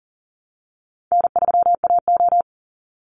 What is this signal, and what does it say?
Morse code, heard as a keyed tone, about 40 wpm, 700 Hz. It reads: N3AO
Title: Easy Call Signs Repeated 3x - 40wpm